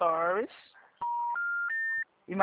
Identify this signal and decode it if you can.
Special information tone, with speech.